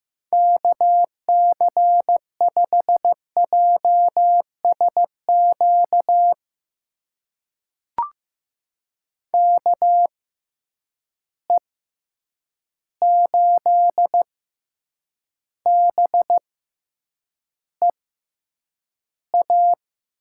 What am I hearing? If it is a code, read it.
Morse code, heard as a keyed tone, about 15 wpm, spaced as for 8 wpm, 700 Hz. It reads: KC5JSQ KE8BEA
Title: USA Call Signs - 6 Characters - 6x Character Spacing - 15wpm